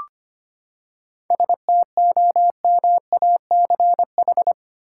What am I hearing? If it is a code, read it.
Morse code, heard as a keyed tone, about 25 wpm, 700 Hz. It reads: STOMACH
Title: Seven Letter Words - 25wpm